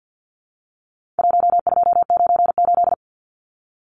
Morse code, heard as a keyed tone, about 49 wpm, 700 Hz. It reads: 1298
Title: Numbers - 4-Digit 50wpm